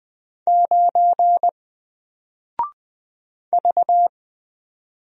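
Morse code, heard as a keyed tone, about 20 wpm, 700 Hz. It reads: 9 V
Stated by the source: Single letter and number Morse drill at 20 wpm, from N4LCD.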